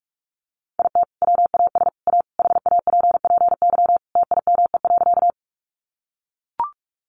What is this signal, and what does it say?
Morse code, heard as a keyed tone, about 44 wpm, 700 Hz. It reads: IT WAS A HAPPY TIME.
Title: Sentences from Top 1000 Words 45wpm